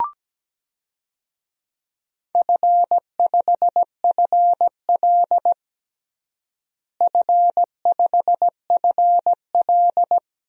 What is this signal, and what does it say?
Morse code, heard as a keyed tone, about 17 wpm, 700 Hz. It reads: F5FL F5FL
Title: DXCC-100 Call Signs - 4 Characters - Repeated 3x - 17wpm